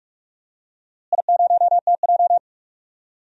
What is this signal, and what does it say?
Morse code, heard as a keyed tone, about 43 wpm, 700 Hz. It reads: I0TJ
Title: DXCC-100 Call Signs - 4 Characters - Repeated 3x - 45wpm